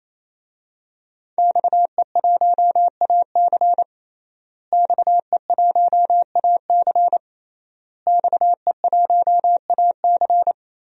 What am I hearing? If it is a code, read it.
Morse code, heard as a keyed tone, about 27 wpm, 700 Hz. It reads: XE1AC XE1AC XE1AC
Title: DXCC-100 Call Signs - 5 Characters - Repeated 3x - 28wpm